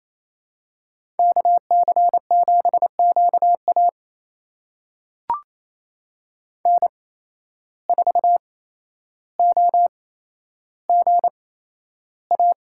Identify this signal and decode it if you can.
Morse code, heard as a keyed tone, about 28 wpm, spaced as for 12 wpm, 700 Hz. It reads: KC7QA N4OGA
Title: USA Call Signs - 5 Characters - 8x Character Spacing - 28wpm